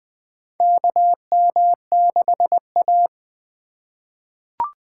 Morse code, heard as a keyed tone, about 20 wpm, 700 Hz. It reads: KM6A
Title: USA Call Signs - 4 Characters - 4x Character Spacing - 20wpm